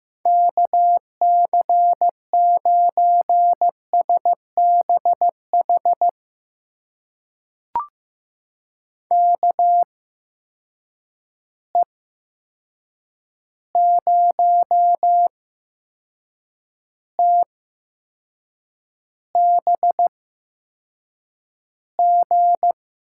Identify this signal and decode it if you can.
Morse code, heard as a keyed tone, about 15 wpm, spaced as for 6 wpm, 700 Hz. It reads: KC9SBH KE0TBG
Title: USA Call Signs - 6 Characters - 8x Character Spacing - 15wpm